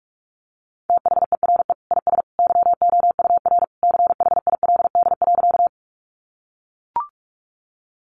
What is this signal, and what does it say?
Morse code, heard as a keyed tone, about 44 wpm, 700 Hz. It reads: THERE IS YOUR CHILD.
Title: Sentences from Top 800 Words 45wpm